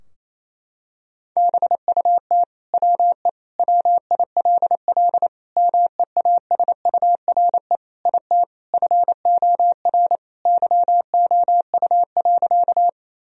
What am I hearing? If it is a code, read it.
Morse code, heard as a keyed tone, about 28 wpm, 700 Hz. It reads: BUT WE WILL MEASURE IT FOR YOU.
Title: Sentences from Top 1000 Words - Review Part 2 - 28wpm